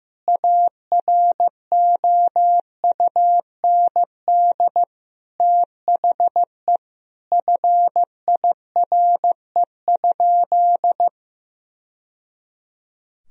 Morse code, heard as a keyed tone, about 15 wpm, 700 Hz. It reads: AROUND THE FIRE?
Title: Sentences from Top 1000 Words - Review Part 3 - 15wpm